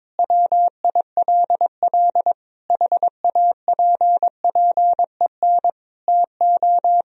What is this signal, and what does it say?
Morse code, heard as a keyed tone, about 22 wpm, 700 Hz. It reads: WILL HAPPEN TO
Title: Sentences from Top 500 Words and Abbreviations 22wpm